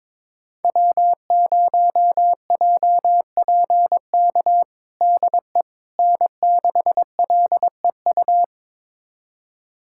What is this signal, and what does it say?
Morse code, heard as a keyed tone, about 22 wpm, 700 Hz. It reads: W0JPK DE N6LEU
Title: USA Callsign DE Callsign - 5 Characters - 22wpm